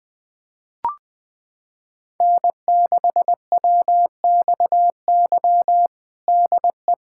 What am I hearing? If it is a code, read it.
Morse code, heard as a keyed tone, about 21 wpm, 700 Hz. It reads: N6WXY DE
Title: USA Callsign DE Callsign - 5 Characters - 20wpm